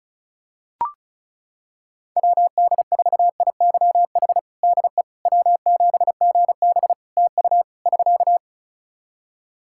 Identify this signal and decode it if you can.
Morse code, heard as a keyed tone, about 36 wpm, 700 Hz. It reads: WD4IYH DE W7GB TU <SK>
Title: Practice QSO - Exchange 4 - Ending - 35wpm